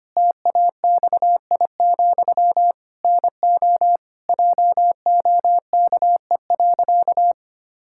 Morse code, heard as a keyed tone, about 25 wpm, 700 Hz. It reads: TAXI, NO JOKE.
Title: Pangram Challenge v2 - 25wpm